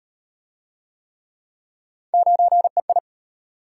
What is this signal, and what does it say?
Morse code, heard as a keyed tone, about 35 wpm, 700 Hz. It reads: 9EI